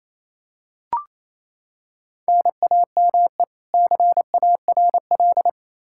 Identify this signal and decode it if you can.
Morse code, heard as a keyed tone, about 28 wpm, 700 Hz. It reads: NAME CARL
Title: Practice QSO - Exchange 1 - Beginning - 28wpm